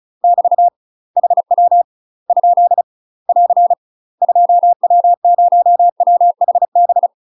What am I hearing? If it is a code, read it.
Morse code, heard as a keyed tone, about 34 wpm, 700 Hz. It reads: <BT> HW ? <AR> 2W0WHB
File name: UK MASTER QSOs TRAINING / 35WPM / UK MASTER QSOs 0005